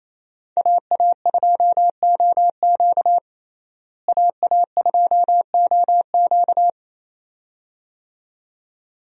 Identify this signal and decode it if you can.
Morse code, heard as a keyed tone, about 28 wpm, 700 Hz. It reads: AA2OQ AA2OQ
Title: USA Call Signs - 5 Characters - Repeated 2x - 28wpm